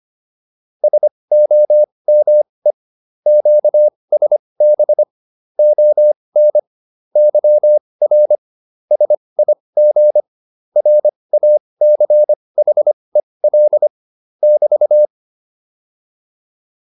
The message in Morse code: SOME QSB ON YR SIG RACHEL <BT>

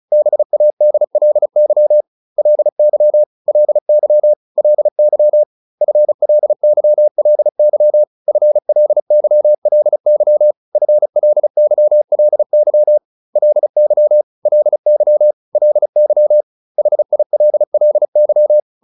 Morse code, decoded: BADLY LY LY LY FLYLY FLYLY FLYLY LY LY LY HILLY

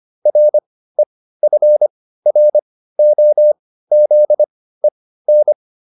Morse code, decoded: REFROZEN

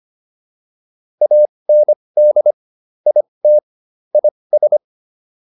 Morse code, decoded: AND IT IS

